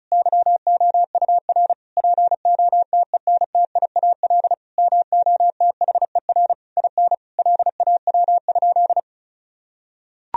Morse code, heard as a keyed tone, about 34 wpm, 700 Hz. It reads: YOUR POTENTIAL MOTHER IN LAW?
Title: Sentences from Top 1000 Words - Review Part 4 - 35wpm